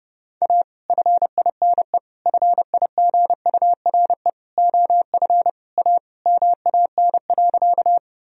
Morse code, heard as a keyed tone, about 32 wpm, 700 Hz. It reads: A FINE FIGURE OF A MAN.
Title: Sentences from Top 1000 Words - Review Part 2 - 30wpm